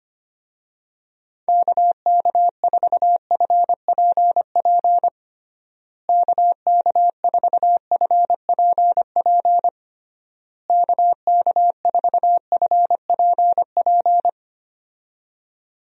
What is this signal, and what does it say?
Morse code, heard as a keyed tone, about 25 wpm, 700 Hz. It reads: KK4FPP KK4FPP KK4FPP
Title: USA Call Signs - 6 Characters - Repeated 3x - 25wpm